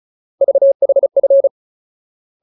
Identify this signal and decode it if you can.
Morse code, heard as a keyed tone, about 34 wpm, 550 Hz. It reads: VHF
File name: Ultimate Word List / 35WPM Ultimate